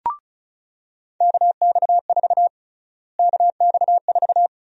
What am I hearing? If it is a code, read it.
Morse code, heard as a keyed tone, about 36 wpm, 700 Hz. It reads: KX4 KX4
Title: USA Call Sign Prefixes - Repeated 2x - 35wpm